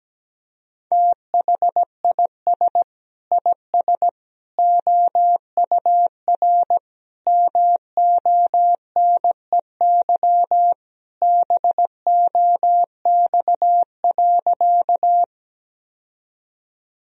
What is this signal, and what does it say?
Morse code, heard as a keyed tone, about 17 wpm, 700 Hz. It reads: THIS IS OUR MONEY BOX.
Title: Sentences from Top 1000 Words - Review Part 2 - 17wpm